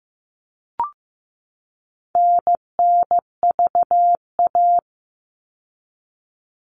Morse code, heard as a keyed tone, about 15 wpm, 700 Hz. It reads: NNVA
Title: All Cut Numbers - 4-Digit 15wpm